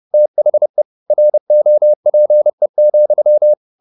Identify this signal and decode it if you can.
Morse code, heard as a keyed tone, about 30 wpm, 600 Hz. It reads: THE ROPE,